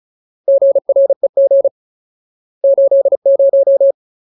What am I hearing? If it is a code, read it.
Morse code, heard as a keyed tone, about 36 wpm, 550 Hz. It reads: GREG 80
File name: Ultimate Word List / 35WPM Ultimate